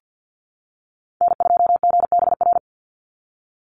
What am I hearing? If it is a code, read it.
Morse code, heard as a keyed tone, about 48 wpm, 700 Hz. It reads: N2GBR
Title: NA SOTA Activator Callsigns 50wpm